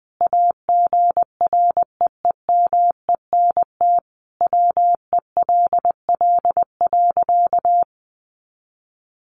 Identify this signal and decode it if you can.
Morse code, heard as a keyed tone, about 19 wpm, 700 Hz. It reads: AGREEMENT WELL.
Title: Sentences from Top 1500 Words 20wpm